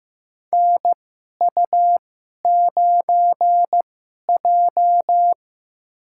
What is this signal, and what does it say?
Morse code, heard as a keyed tone, about 15 wpm, spaced as for 11 wpm, 700 Hz. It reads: NU9J
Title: USA Call Signs - 2x Character Spacing - 15wpm